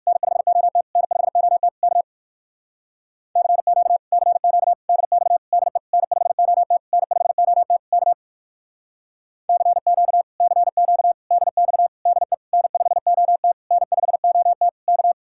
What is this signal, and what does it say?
Morse code, heard as a keyed tone, about 64 wpm, 700 Hz. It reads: N5OT N5OT K CQ CQ DX DE N5OT N5OT K CQ CQ DX DE N5OT N5OT K